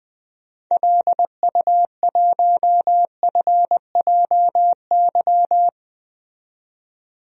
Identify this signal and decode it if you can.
Morse code, heard as a keyed tone, about 20 wpm, 700 Hz. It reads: LU1FJY
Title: DXCC-100 Call Signs - 6 Characters - 20wpm